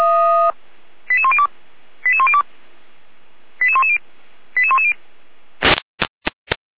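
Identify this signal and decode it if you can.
Five-tone selective call (ZVEI), heard as quick runs of five tones, tones of about 70 ms. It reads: 88102 88102 88110 88110
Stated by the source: ZVEI five-tone call 88129 received by SDR, channel 455, Feuerwehr Orlamuende.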